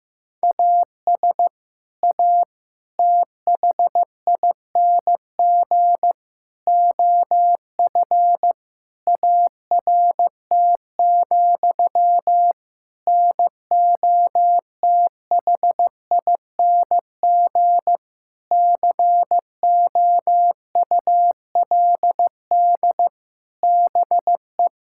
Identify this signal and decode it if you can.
Morse code, heard as a keyed tone, about 15 wpm, 700 Hz. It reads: AS A THING OF ART, NOTHING COULD BE
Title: Sentences from Top 1000 Words - Review Part 6 - 15wpm